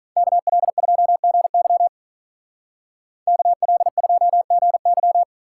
Morse code, heard as a keyed tone, about 43 wpm, 700 Hz. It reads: KL2GY KL2GY
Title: Speed Racing - USA 2x2 Call Signs 35wpm